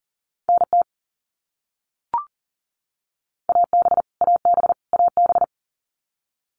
Morse code, heard as a keyed tone, about 40 wpm, 700 Hz. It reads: NT AB AB AB
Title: Canadian Provinces and Territories - Repeated 3x - 40wpm